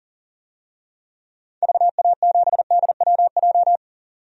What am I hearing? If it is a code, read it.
Morse code, heard as a keyed tone, about 40 wpm, 700 Hz. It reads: VA7DWJ
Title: DXCC-100 Call Signs - 6 Characters - 40wpm